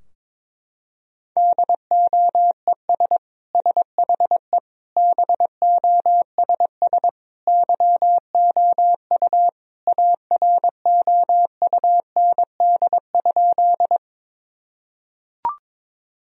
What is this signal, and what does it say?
Morse code, heard as a keyed tone, about 22 wpm, 700 Hz. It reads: DOES SHE BOSS YOU AROUND?
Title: Sentences from Top 1000 Words - Review Part 4 - 22wpm